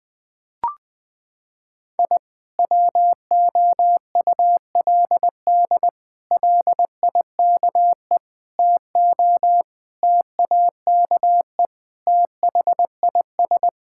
Morse code, heard as a keyed tone, about 21 wpm, 700 Hz. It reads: I WOULD LIKE TO TAKE THIS
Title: Sentences with 8-Word Phrases - 20wpm